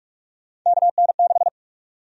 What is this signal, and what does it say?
Morse code, heard as a keyed tone, about 46 wpm, 700 Hz. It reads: KN6
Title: USA Call Sign Prefixes - Repeated 2x - 45wpm